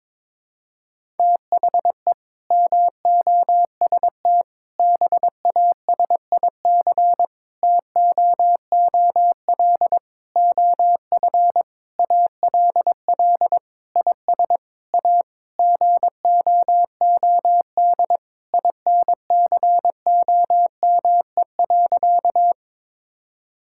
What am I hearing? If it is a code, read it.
Morse code, heard as a keyed tone, about 21 wpm, 700 Hz. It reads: THE MOST BASIC TOOL OF ALL IS A GOOD INCOME.
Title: Sentences from Top 1000 Words - Review Part 6 - 22wpm